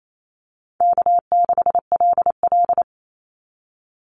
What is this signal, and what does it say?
Morse code, heard as a keyed tone, about 28 wpm, 700 Hz. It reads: K6LL
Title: Call Signs Repeated 3x - 28wpm